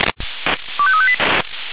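Five-tone selective call (ZVEI), heard as quick runs of five tones, tones of about 70 ms. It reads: 25358
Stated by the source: ZVEI five-tone selective call received on SDR.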